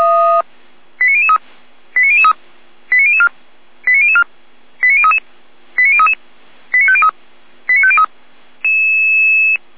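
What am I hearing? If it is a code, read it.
Five-tone selective call (ZVEI), heard as quick runs of five tones, tones of about 70 ms. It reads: 89003 89003 89004 89004 89033 89033 89503 89503
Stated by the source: ZVEI five-tone call 89001 received on an SDR.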